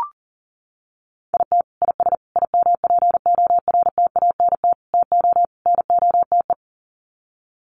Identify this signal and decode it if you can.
Morse code, heard as a keyed tone, about 40 wpm, 700 Hz. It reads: IT IS IMPORTANT TO NOTE
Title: Sets of 5 Words 40wpm